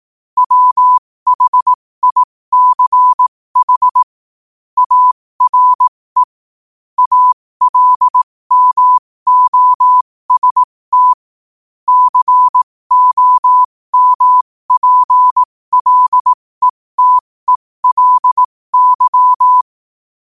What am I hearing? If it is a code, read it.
Morse code, heard as a keyed tone, about 18 wpm, spaced as for 15 wpm, 1000 Hz. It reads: WHICH ARE ALMOST COMPLETELY
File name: qotdfiles / 15wpm